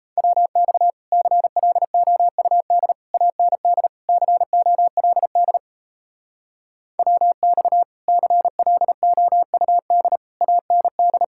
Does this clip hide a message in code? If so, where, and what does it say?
Yes: Morse code from 0.2 to 11.2 s, heard as a keyed tone, about 36 wpm, 700 Hz. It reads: WX CLOUD AND COLD WX CLOUD AND